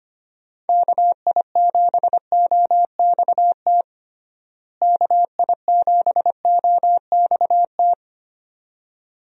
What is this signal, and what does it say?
Morse code, heard as a keyed tone, about 25 wpm, 700 Hz. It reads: KI7OXT KI7OXT
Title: USA Call Signs - 6 Characters - Repeated 2x - 25wpm